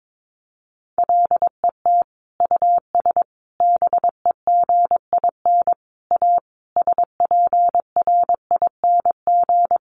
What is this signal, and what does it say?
Morse code, heard as a keyed tone, about 22 wpm, 700 Hz. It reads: LET US BEGIN A SPRING
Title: Sentences from Top 1400 Words 22wpm